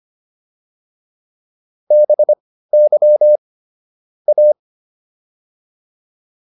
Morse code, heard as a keyed tone, about 25 wpm, spaced as for 15 wpm, 600 Hz. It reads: BY A